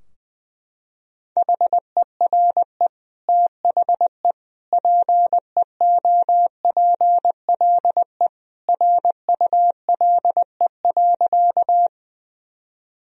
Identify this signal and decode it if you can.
Morse code, heard as a keyed tone, about 19 wpm, 700 Hz. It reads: HERE THE PEOPLE RULE.
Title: Sentences from Top 1000 Words - Review Part 2 - 20wpm